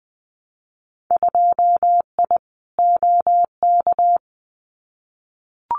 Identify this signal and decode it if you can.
Morse code, heard as a keyed tone, about 20 wpm, 700 Hz. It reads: 2I OK
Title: Winter Field Day Exchanges Encore 20wpm